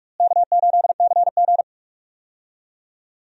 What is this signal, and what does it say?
Morse code, heard as a keyed tone, about 46 wpm, 700 Hz. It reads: K8CG
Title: USA Call Signs - 4 Characters - Repeated 3x - 45wpm